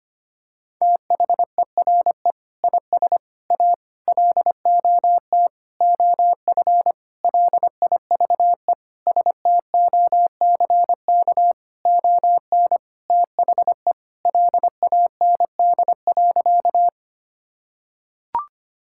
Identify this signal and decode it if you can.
Morse code, heard as a keyed tone, about 25 wpm, 700 Hz. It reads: THERE IS A LOT OF LIVE STOCK ON THE LAND.
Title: Sentences from Top 1000 Words - Review Part 5 - 25wpm